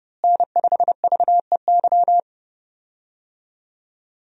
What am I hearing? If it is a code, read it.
Morse code, heard as a keyed tone, about 30 wpm, 700 Hz. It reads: N5VEY
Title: USA Call Signs - 5 Characters - Repeated 3x - 30wpm